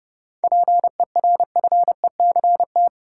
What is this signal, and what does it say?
Morse code, heard as a keyed tone, about 30 wpm, 700 Hz. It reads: PERFECT
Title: Seven Letter Words - 30wpm